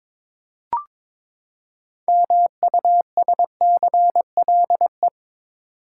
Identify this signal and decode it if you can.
Morse code, heard as a keyed tone, about 23 wpm, 700 Hz. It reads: MUSCLE
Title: Six Letter Words - 22wpm